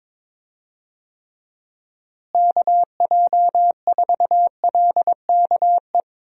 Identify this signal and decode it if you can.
Morse code, heard as a keyed tone, about 22 wpm, 700 Hz. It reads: KJ4LKE
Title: USA Call Signs - 6 Characters - Repeated 2x - 22wpm